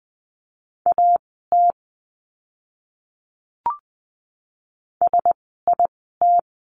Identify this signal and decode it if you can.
Morse code, heard as a keyed tone, about 20 wpm, spaced as for 14 wpm, 700 Hz. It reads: AT SIT
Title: 115 Lesson 19f - Letter w - Words 20wpm